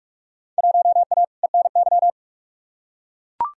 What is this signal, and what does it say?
Morse code, heard as a keyed tone, about 44 wpm, 700 Hz. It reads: 1A ENY
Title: Field Day Exchanges - 45wpm